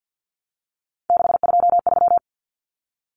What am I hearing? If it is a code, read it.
Morse code, heard as a keyed tone, about 48 wpm, 700 Hz. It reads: 623
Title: Common Cut Numbers - 3-Digit 50wpm